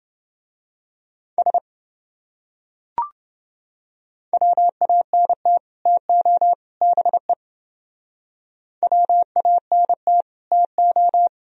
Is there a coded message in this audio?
Yes: Morse code from 1.4 to 11.3 s, heard as a keyed tone, about 28 wpm, 700 Hz. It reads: S WANT TO BE WANT TO